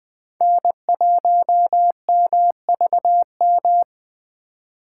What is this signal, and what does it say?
Morse code, heard as a keyed tone, about 20 wpm, 700 Hz. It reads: N1MVM
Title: USA Call Signs - 5 Characters - 8x Character Spacing - 20wpm